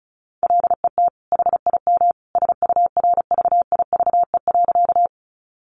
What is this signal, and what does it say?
Morse code, heard as a keyed tone, about 34 wpm, 700 Hz. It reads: LET HIM SURVIVE.
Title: Sentences from Top 1300 Words 35wpm